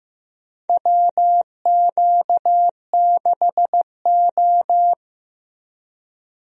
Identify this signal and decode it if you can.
Morse code, heard as a keyed tone, about 16 wpm, 700 Hz. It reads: WQ6O
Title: USA Call Signs - 4 Characters - Repeated 3x - 15wpm